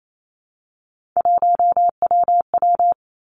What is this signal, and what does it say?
Morse code, heard as a keyed tone, about 28 wpm, 700 Hz. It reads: 1WW
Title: Three Challenging Letter-Number Combinations 28wpm